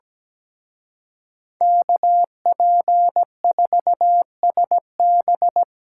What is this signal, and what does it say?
Morse code, heard as a keyed tone, about 20 wpm, 700 Hz. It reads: KP4SB
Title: Speed Racing - USA 2x2 Call Signs 17wpm